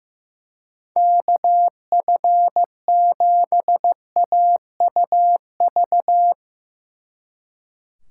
Morse code, heard as a keyed tone, about 15 wpm, 700 Hz. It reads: KF7AUV
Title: Speed Racing - USA 2x3 Call Signs 15wpm